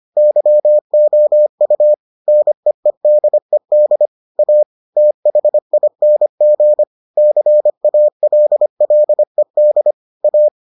Morse code, heard as a keyed tone, about 25 wpm, 600 Hz. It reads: YOU NEEDED A THING CALLED A